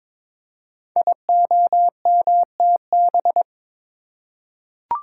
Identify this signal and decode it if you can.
Morse code, heard as a keyed tone, about 22 wpm, 700 Hz. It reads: IOMTB
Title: Code Groups - 5 Letters - 22wpm